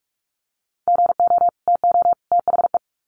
Morse code, heard as a keyed tone, about 44 wpm, 700 Hz. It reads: GO TO THE
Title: Sets of 3 Words 45wpm